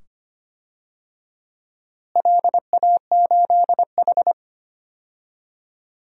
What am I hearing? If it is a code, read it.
Morse code, heard as a keyed tone, about 25 wpm, 700 Hz. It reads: LA8H
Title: DXCC-100 Call Signs - 4 Characters - Repeated 2x - 25wpm